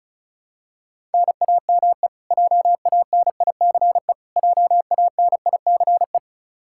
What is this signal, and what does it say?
Morse code, heard as a keyed tone, about 36 wpm, 700 Hz. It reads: NAME JANICE JANICE
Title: Practice QSO - Exchange 1 - Beginning - 35wpm